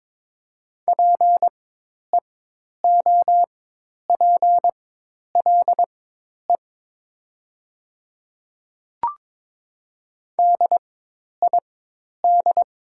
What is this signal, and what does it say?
Morse code, heard as a keyed tone, about 22 wpm, spaced as for 10 wpm, 700 Hz. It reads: PEOPLE DID
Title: 4x Character Spacing - Top 100 Words - 22wpm